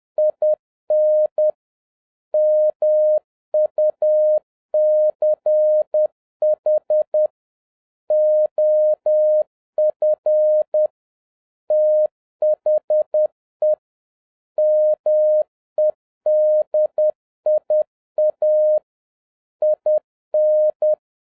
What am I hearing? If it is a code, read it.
Morse code, heard as a keyed tone, about 10 wpm, 610 Hz. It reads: IN MUCH OF THE MEDIA IN